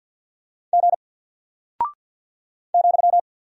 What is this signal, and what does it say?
Morse code, heard as a keyed tone, about 48 wpm, 700 Hz. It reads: G ,
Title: Single Character (Letter:Number:Punctuation:Prosign) Alt 50wpm